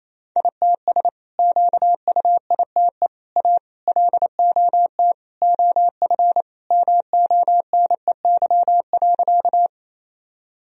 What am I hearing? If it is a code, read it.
Morse code, heard as a keyed tone, about 28 wpm, 700 Hz. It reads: ITS QUITE A LOT OF MONEY.
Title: Sentences from Top 1000 Words - Review Part 3 - 28wpm